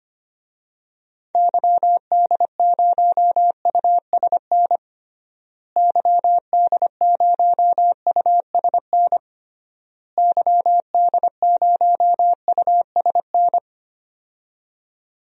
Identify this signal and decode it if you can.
Morse code, heard as a keyed tone, about 25 wpm, 700 Hz. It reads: YD0USN YD0USN YD0USN
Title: DXCC-100 Call Signs - 6 Characters - Repeated 3x - 25wpm